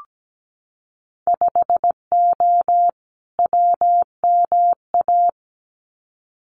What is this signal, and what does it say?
Morse code, heard as a keyed tone, about 17 wpm, 700 Hz. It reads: 5O WMA
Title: Winter Field Day Exchanges Encore 17wpm